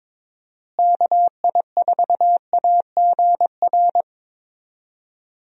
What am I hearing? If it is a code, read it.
Morse code, heard as a keyed tone, about 22 wpm, 700 Hz. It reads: KI4AGR
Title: USA Call Signs - 6 Characters - Repeated 2x - 22wpm